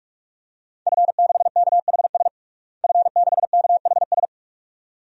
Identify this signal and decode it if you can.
Morse code, heard as a keyed tone, about 48 wpm, 700 Hz. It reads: F6KHS F6KHS
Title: DXCC-100 Call Signs - 5 Characters - Repeated 2x - 45wpm